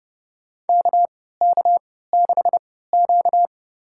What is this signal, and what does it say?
Morse code, heard as a keyed tone, about 32 wpm, spaced as for 17 wpm, 700 Hz. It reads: KK6Q
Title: USA Call Signs - 4 Characters - 3x Character Spacing - 30wpm